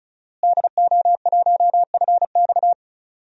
Morse code, heard as a keyed tone, about 36 wpm, 700 Hz. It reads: DO1FX
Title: DXCC-100 Call Signs - 5 Characters - Repeated 3x - 35wpm